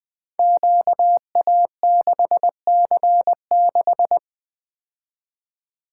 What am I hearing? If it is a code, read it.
Morse code, heard as a keyed tone, about 21 wpm, 700 Hz. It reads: QA6C6
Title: Code Groups - 5 Letter:Numbers - 20wpm